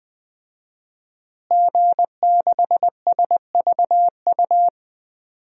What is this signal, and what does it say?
Morse code, heard as a keyed tone, about 21 wpm, 700 Hz. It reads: G6SVU
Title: DXCC-100 Call Signs - 5 Characters - Repeated 3x - 20wpm